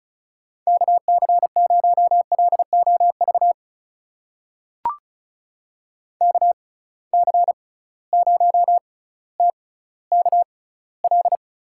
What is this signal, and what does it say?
Morse code, heard as a keyed tone, about 36 wpm, spaced as for 18 wpm, 700 Hz. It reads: KC0LOV KC0TKL
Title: USA Call Signs - 6 Characters - 6x Character Spacing - 35wpm